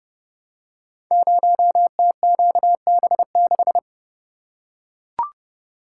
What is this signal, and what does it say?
Morse code, heard as a keyed tone, about 30 wpm, 700 Hz. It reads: 0TQB6
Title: Code Groups - 5 Letter:Numbers - 30wpm